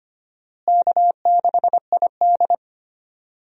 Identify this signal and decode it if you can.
Morse code, heard as a keyed tone, about 25 wpm, 700 Hz. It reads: K6ID
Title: USA Call Signs - 4 Characters - Repeated 2x - 25wpm